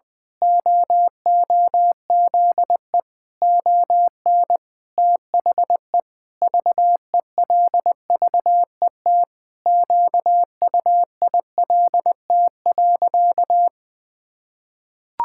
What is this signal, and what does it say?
Morse code, heard as a keyed tone, about 19 wpm, 700 Hz. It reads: OOZE ON THE VELVET QUILT.
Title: Pangram Challenge v2 - 20wpm